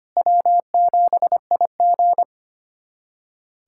Morse code, heard as a keyed tone, about 25 wpm, 700 Hz. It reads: W7IG
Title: USA Callsign DE Callsign - 4 Characters - 25wpm